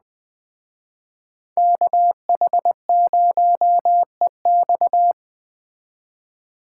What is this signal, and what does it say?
Morse code, heard as a keyed tone, about 21 wpm, 700 Hz. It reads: KH0EX
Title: Speed Racing - USA 2x2 Call Signs 17wpm